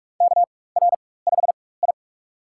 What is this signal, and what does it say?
Morse code, heard as a keyed tone, about 43 wpm, spaced as for 21 wpm, 700 Hz. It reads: KR5I
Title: USA Call Signs - 4 Characters - 4x Character Spacing - 45wpm